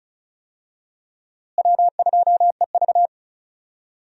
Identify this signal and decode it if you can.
Morse code, heard as a keyed tone, about 34 wpm, 700 Hz. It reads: W2EV
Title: USA Call Signs - 4 Characters - Repeated 2x - 35wpm